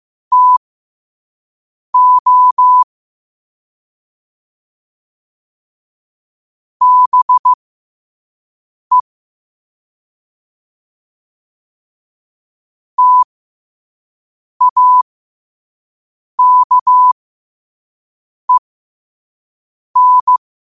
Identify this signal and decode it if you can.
Morse code, heard as a keyed tone, about 15 wpm, spaced as for 5 wpm, 1000 Hz. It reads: TO BE TAKEN
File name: qotdfiles / 5wpm